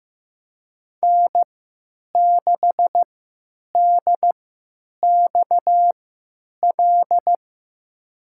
Morse code, heard as a keyed tone, about 15 wpm, spaced as for 9 wpm, 700 Hz. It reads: N6DXL
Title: USA Call Signs - 5 Characters - 3x Character Spacing - 15wpm